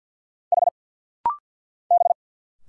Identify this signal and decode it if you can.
Morse code, heard as a keyed tone, about 50 wpm, 700 Hz. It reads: H B
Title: Single Letter-Number Alt - Rapid-Fire 50wpm